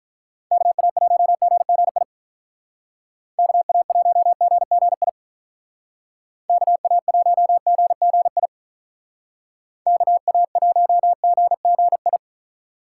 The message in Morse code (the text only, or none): KA1GGI KA1GGI KA1GGI KA1GGI